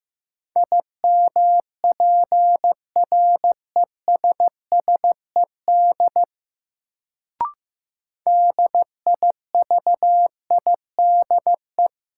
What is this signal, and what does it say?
Morse code, heard as a keyed tone, about 15 wpm, 700 Hz. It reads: IMPRESSED DIVIDE
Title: Sentences from Top 1600 Words - New Word Drill - 15wpm